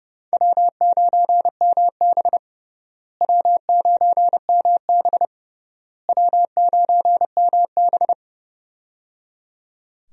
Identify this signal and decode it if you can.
Morse code, heard as a keyed tone, about 30 wpm, 700 Hz. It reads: W9MB W9MB W9MB
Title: USA Call Signs - 4 Characters - Repeated 3x - 30wpm